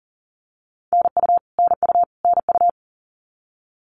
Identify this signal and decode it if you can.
Morse code, heard as a keyed tone, about 40 wpm, 700 Hz. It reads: NU NU NU
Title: Canadian Provinces and Territories - Repeated 3x - 40wpm